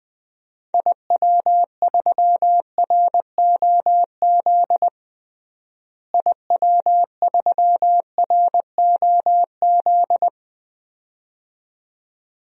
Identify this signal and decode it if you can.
Morse code, heard as a keyed tone, about 21 wpm, 700 Hz. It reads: IW3ROZ IW3ROZ
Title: DXCC-100 Call Signs - 6 Characters - Repeated 2x - 20wpm